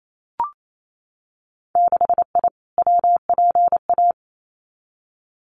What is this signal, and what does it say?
Morse code, heard as a keyed tone, about 28 wpm, 700 Hz. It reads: 6I WPA
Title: Winter Field Day Exchanges Encore 28wpm